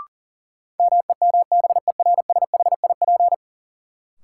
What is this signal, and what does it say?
Morse code, heard as a keyed tone, about 40 wpm, 700 Hz. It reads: MEMBERSHIP
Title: Sentences from Top 1400 Words - New Word Drill - 40wpm